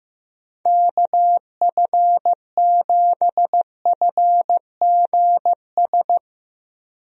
Morse code, heard as a keyed tone, about 15 wpm, 700 Hz. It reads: KF7FGS
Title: USA Call Signs - 6 Characters - Repeated 3x - 15wpm